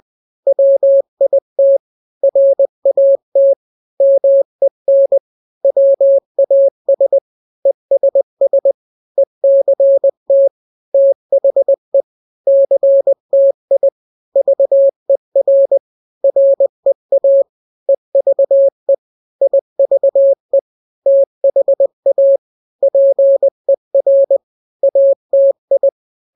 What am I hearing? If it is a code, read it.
Morse code, heard as a keyed tone, about 20 wpm, 550 Hz. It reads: WIT RAT MEN WAS ESS ECT THE CTI VER REA EVE IVE THA PER ATI
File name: n-grams / 3-grams 20 18 WPM